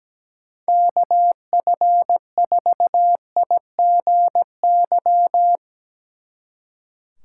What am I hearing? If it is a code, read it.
Morse code, heard as a keyed tone, about 17 wpm, 700 Hz. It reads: KF4IGY
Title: Speed Racing - USA 2x3 Call Signs 17wpm